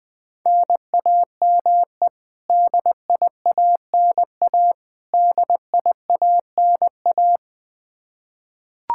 Morse code, heard as a keyed tone, about 20 wpm, 700 Hz. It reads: NAME DIANA DIANA
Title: Practice QSO - Exchange 1 - Beginning - 20wpm